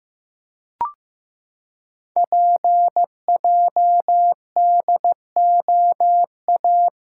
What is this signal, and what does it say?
Morse code, heard as a keyed tone, about 15 wpm, 690 Hz. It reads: PJDOA
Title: Code Groups - 5 Letters - 15wpm